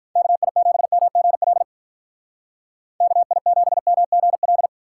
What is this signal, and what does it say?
Morse code, heard as a keyed tone, about 43 wpm, 700 Hz. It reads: KI7MGL KI7MGL
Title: Speed Racing - USA 2x3 Call Signs 35wpm